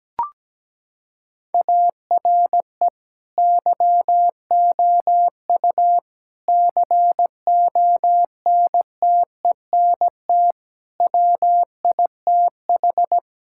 Morse code, heard as a keyed tone, about 17 wpm, 700 Hz. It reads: ARE YOU CONTENT WITH
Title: Sentences from Top 1000 Words - Review Part 6 - 17wpm